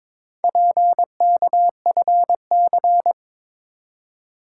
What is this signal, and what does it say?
Morse code, heard as a keyed tone, about 22 wpm, 700 Hz. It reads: PKFC
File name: practice / 22wpm / Four Letter-Number Combinations - 22wpm